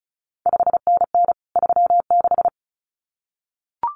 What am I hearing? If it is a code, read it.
Morse code, heard as a keyed tone, about 36 wpm, 700 Hz. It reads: 5NN 36
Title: CQ WW DX Contest Exchange Encore - 35wpm